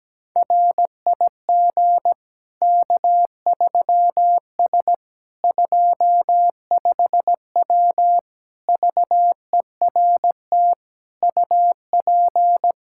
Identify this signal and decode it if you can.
Morse code, heard as a keyed tone, about 17 wpm, 700 Hz. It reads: RIG K3S 25W VERT UP
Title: Practice QSO - Exchange 2 - Info - 17wpm